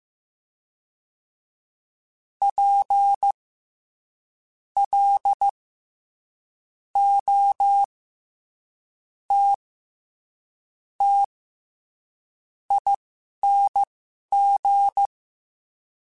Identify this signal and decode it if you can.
Morse code, heard as a keyed tone, about 15 wpm, spaced as for 6 wpm, 800 Hz. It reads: PLOTTING
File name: Special INGs-mp3s / Special INGs-mp3s / SlowToFaster ing @10